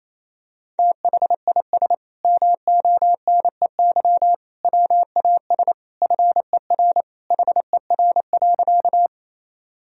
Morse code, heard as a keyed tone, about 28 wpm, 700 Hz. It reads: THIS MONEY WAS FER HER.
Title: Sentences from Top 400 Words and Abbreviations 28wpm